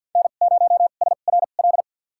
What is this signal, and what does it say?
Morse code, heard as a keyed tone, about 53 wpm, spaced as for 37 wpm, 700 Hz. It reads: N0SRL
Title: USA Call Signs - 5 Characters - 2x Character Spacing - 50wpm